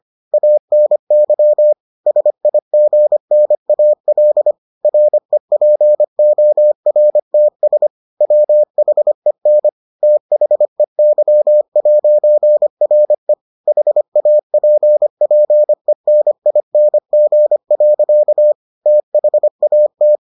Morse code, heard as a keyed tone, about 25 wpm, 600 Hz. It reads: ANY SIGNAL REPORTS WHEN THEY'RE HAPPENING. THAT